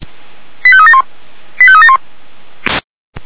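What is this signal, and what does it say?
Five-tone selective call (ZVEI), heard as quick runs of five tones, tones of about 70 ms. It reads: 85371 85371
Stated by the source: ZVEI five-tone selective call 85371 received on SDR.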